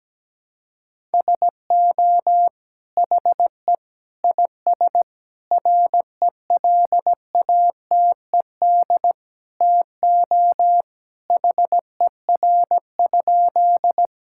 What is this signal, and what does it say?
Morse code, heard as a keyed tone, about 17 wpm, 700 Hz. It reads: SO HE IS RELATED TO HER?
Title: Sentences from Top 1000 Words - Review Part 5 - 17wpm